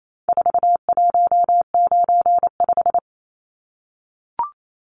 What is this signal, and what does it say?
Morse code, heard as a keyed tone, about 28 wpm, 700 Hz. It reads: 4195
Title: Numbers - 4-Digit 28wpm